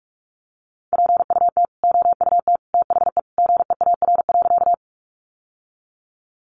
Morse code, heard as a keyed tone, about 44 wpm, 700 Hz. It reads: PUT OUT THE GEAR.
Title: Sentences from Top 1400 Words 45wpm